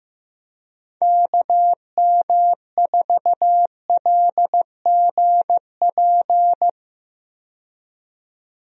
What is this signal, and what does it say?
Morse code, heard as a keyed tone, about 16 wpm, 700 Hz. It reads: KM4LGP
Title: USA Call Signs - 6 Characters - 15wpm